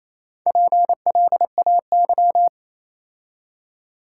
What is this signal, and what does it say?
Morse code, heard as a keyed tone, about 27 wpm, 690 Hz. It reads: PLAY
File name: Easy Words - Speed-Racing with Hints - 22wpm